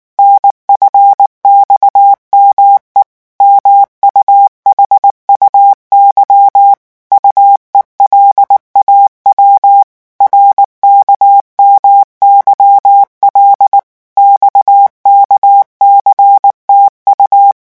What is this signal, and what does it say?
Morse code, heard as a keyed tone, about 19 wpm, 800 Hz. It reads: NFXME MUHUY UELAW RKMYL XKCTU
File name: morse mp3 / revision7-vitesse 18 mots